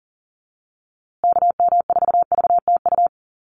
Morse code, heard as a keyed tone, about 40 wpm, 700 Hz. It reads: KM4VTU
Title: USA Call Signs Repeated 2x 40wpm